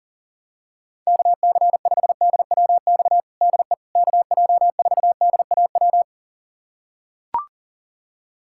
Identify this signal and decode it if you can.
Morse code, heard as a keyed tone, about 40 wpm, 700 Hz. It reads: KC5DWX DE KJ4DAW
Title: USA Callsign DE Callsign - 6 Characters - 40wpm